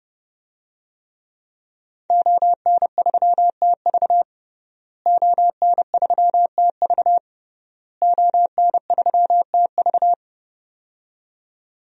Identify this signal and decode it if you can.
Morse code, heard as a keyed tone, about 30 wpm, 700 Hz. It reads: ON3TV ON3TV ON3TV
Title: DXCC-100 Call Signs - 5 Characters - Repeated 3x - 30wpm